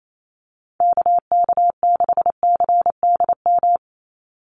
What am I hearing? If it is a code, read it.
Morse code, heard as a keyed tone, about 28 wpm, 700 Hz. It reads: KK6CDM
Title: USA Call Signs 28wpm